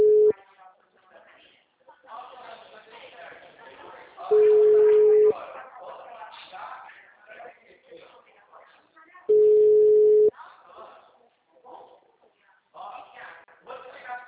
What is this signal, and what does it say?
Ringback tone (Europe, 425 Hz), with speech.